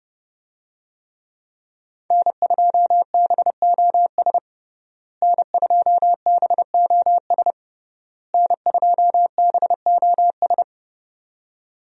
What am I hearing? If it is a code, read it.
Morse code, heard as a keyed tone, about 32 wpm, 700 Hz. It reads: N2BOS N2BOS N2BOS
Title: USA Call Signs - 5 Characters - Repeated 3x - 30wpm